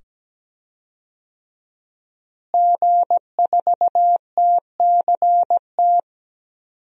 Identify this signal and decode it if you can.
Morse code, heard as a keyed tone, about 17 wpm, 700 Hz. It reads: G4TCT
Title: DXCC-100 Call Signs - 5 Characters - Repeated 2x - 17wpm